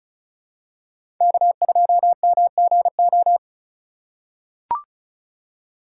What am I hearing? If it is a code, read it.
Morse code, heard as a keyed tone, about 36 wpm, 700 Hz. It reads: K2MGO
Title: USA Call Signs - 5 Characters - 3x Character Spacing - 35wpm